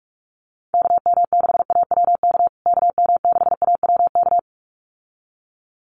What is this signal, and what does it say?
Morse code, heard as a keyed tone, about 44 wpm, 700 Hz. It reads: KM6AWK KM6AWK
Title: USA Call Signs Repeated 2x 45wpm